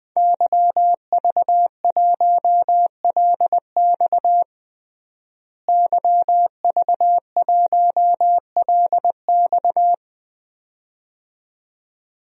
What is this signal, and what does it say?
Morse code, heard as a keyed tone, about 21 wpm, 700 Hz. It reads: YV1LX YV1LX
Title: DXCC-100 Call Signs - 5 Characters - Repeated 2x - 20wpm